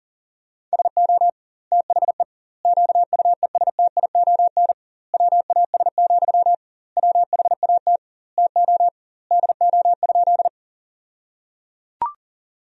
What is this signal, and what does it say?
Morse code, heard as a keyed tone, about 37 wpm, 700 Hz. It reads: SO THE QUESTION WAS, WHAT TO DO?
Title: 2x Word-Spacing - Sentences from Top 400 Words - 40wpm